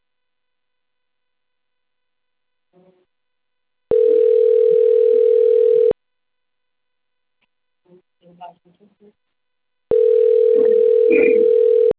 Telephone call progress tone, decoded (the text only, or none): ringback tone